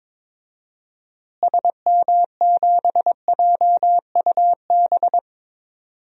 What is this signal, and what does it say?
Morse code, heard as a keyed tone, about 22 wpm, 700 Hz. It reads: SM7JUB
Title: DXCC-100 Call Signs - 6 Characters - Repeated 2x - 22wpm